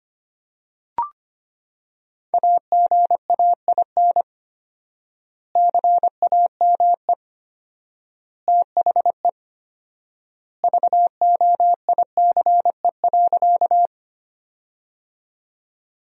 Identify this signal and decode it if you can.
Morse code, heard as a keyed tone, about 25 wpm, 700 Hz. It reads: AGAIN CAME THE VOICE.
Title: 4x Word-Spacing - Sentences from Top 500 Words - 25wpm